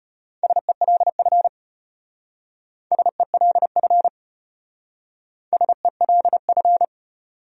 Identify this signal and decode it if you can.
Morse code, heard as a keyed tone, about 32 wpm, 700 Hz. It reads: SELF SELF SELF